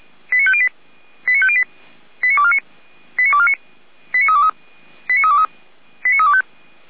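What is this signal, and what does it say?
Five-tone selective call (ZVEI), heard as quick runs of five tones, tones of about 70 ms. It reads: 89508 89508 89249 89249 89323 89323 89325
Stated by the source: ZVEI five-tone selective call received on an SDR, from the Feuerwehr Orlamünde (f85360) fire brigade audio server.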